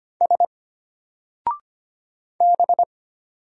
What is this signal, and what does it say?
Morse code, heard as a keyed tone, about 25 wpm, 700 Hz. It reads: S B
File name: Single-Letter-Number-25wpm